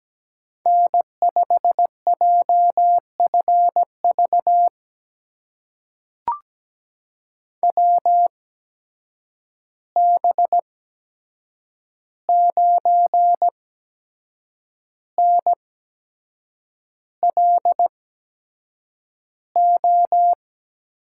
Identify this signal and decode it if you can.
Morse code, heard as a keyed tone, about 17 wpm, spaced as for 7 wpm, 700 Hz. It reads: N5JFV WB9NLO
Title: USA Call Signs - 8x Character Spacing - 17wpm